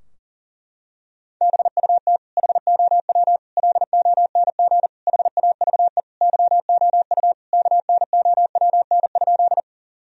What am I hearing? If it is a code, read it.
Morse code, heard as a keyed tone, about 40 wpm, 700 Hz. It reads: BUT HOW LONG HAVE YOU KNOWN?
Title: Sentences from Top 1000 Words - Review Part 3 - 40wpm